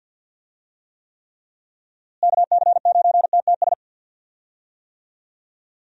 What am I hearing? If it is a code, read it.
Morse code, heard as a keyed tone, about 47 wpm, 700 Hz. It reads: KC9TTS
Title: USA Call Signs - 6 Characters - Repeated 3x - 50wpm